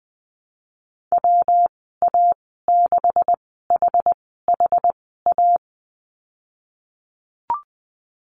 Morse code, heard as a keyed tone, about 20 wpm, spaced as for 15 wpm, 700 Hz. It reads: WA6HHA
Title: 154 Lesson 25g - Letter p - Call Signs 20wpm